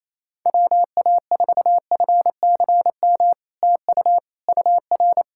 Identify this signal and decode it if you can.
Morse code, heard as a keyed tone, about 28 wpm, 700 Hz. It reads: WA4FCM TU UR
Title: SOTA QSO Practice 28wpm